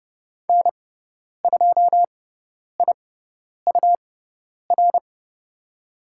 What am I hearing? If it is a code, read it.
Morse code, heard as a keyed tone, about 30 wpm, spaced as for 10 wpm, 700 Hz. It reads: N2IUR
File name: Lesson 18g - Letter m - Call Signs 30wpm